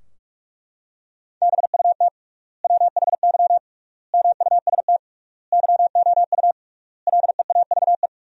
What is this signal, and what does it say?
Morse code, heard as a keyed tone, about 43 wpm, 700 Hz. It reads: BUT WHY MUST YOU LEAVE
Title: 3x Word-Spacing - Sentences from Top 400 Words - 45wpm